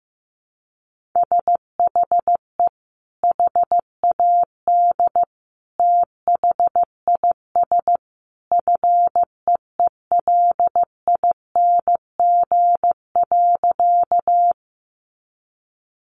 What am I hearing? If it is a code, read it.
Morse code, heard as a keyed tone, about 15 wpm, 700 Hz. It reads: SHE HAD THIS FEELING.
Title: Sentences from Top 1200 Words 15wpm